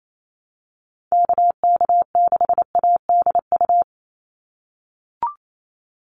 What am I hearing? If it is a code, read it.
Morse code, heard as a keyed tone, about 28 wpm, 700 Hz. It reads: KK6ADU
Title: USA Call Signs 28wpm